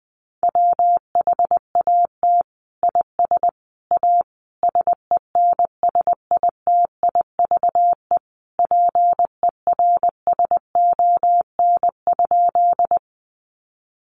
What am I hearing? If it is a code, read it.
Morse code, heard as a keyed tone, about 20 wpm, 700 Hz. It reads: WHAT IS A SENSITIVE PERSON?
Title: Sentences from Top 1400 Words 20wpm